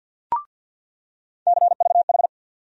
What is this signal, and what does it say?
Morse code, heard as a keyed tone, about 53 wpm, 700 Hz. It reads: CVH
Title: USA Call Sign Postfixes - 3 Letters 50wpm